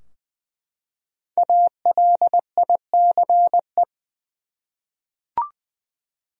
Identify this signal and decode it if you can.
Morse code, heard as a keyed tone, about 21 wpm, 700 Hz. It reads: ALICE
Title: USA Common Names - 20wpm